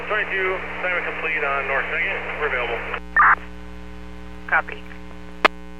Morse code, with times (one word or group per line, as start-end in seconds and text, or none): none